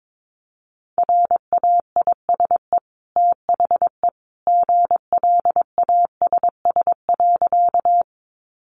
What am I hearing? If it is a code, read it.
Morse code, heard as a keyed tone, about 22 wpm, 700 Hz. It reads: RAISE THE GLASS.
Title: Sentences from Top 1400 Words 22wpm